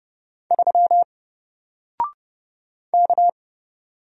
Morse code, heard as a keyed tone, about 30 wpm, 700 Hz. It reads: 3 K
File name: Single-Letter-Number-30wpm